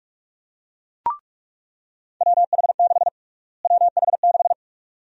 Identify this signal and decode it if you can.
Morse code, heard as a keyed tone, about 43 wpm, 700 Hz. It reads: WH6 WH6
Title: USA Call Sign Prefixes - Repeated 2x - 45wpm